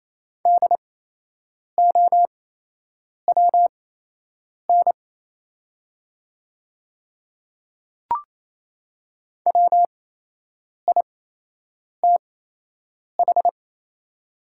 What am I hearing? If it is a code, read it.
Morse code, heard as a keyed tone, about 28 wpm, spaced as for 8 wpm, 700 Hz. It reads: DOWN WITH
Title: 8x Character Spacing - Top 100 Words - 28wpm